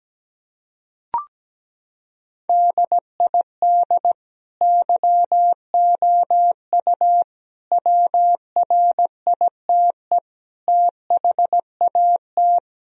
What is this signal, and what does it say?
Morse code, heard as a keyed tone, about 17 wpm, 700 Hz. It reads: DID YOU WRITE THAT
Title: Sentences from Top 1000 Words - Review Part 3 - 17wpm